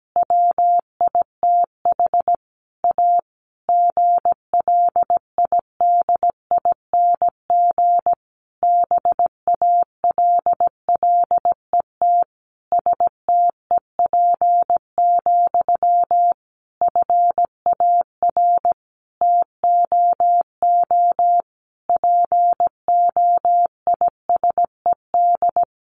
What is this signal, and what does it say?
Morse code, heard as a keyed tone, about 17 wpm, 700 Hz. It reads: WITH A GLIDING BALLET STEP, FAR TOO POISED
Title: A Sound Of Thunder 17wpm